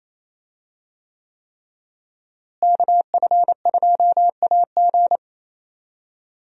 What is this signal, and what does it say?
Morse code, heard as a keyed tone, about 28 wpm, 700 Hz. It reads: KF2AG
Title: USA Call Signs - 5 Characters - Repeated 3x - 28wpm